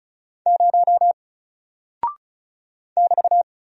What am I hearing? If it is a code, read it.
Morse code, heard as a keyed tone, about 36 wpm, 700 Hz. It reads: 0 <BT>
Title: Single Character (Letter:Number:Punctuation:Prosign) Alt 35wpm